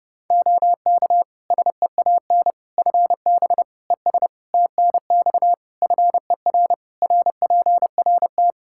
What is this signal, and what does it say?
Morse code, heard as a keyed tone, about 30 wpm, 700 Hz. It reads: OK SEAN FB ES TNX FER RPRT
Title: Practice QSO - Exchange 2 - Info - 30wpm